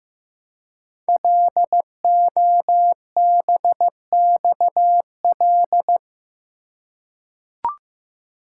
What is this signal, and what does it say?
Morse code, heard as a keyed tone, about 15 wpm, 700 Hz. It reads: LOBXL
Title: Code Groups - 5 Letters - 15wpm